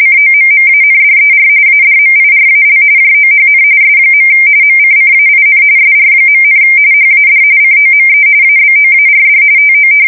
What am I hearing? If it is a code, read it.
RTTY, about 45.45 baud. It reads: COMMANDER ADMIRAL JOHN S. MCCAIN SR., HAD COME ABOARD TO
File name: fsk170 / battleshipnj